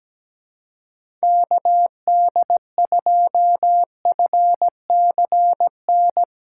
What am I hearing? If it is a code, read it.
Morse code, heard as a keyed tone, about 17 wpm, 700 Hz. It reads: KD2FCN
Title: USA Call Signs - 6 Characters - 2x Character Spacing - 17wpm